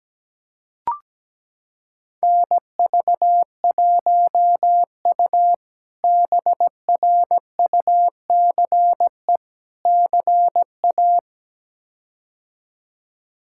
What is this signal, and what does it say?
Morse code, heard as a keyed tone, about 17 wpm, 700 Hz. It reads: NV1U BRUCE CA
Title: K1USN SST QSO Practice - 17wpm